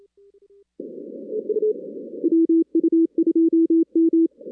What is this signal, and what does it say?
Morse code, heard as a keyed tone, about 27 wpm, 330 Hz. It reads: WU2M
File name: 2016 arrl dx cw / 2016-ARRL-DX-CW-K5TR- - 0-41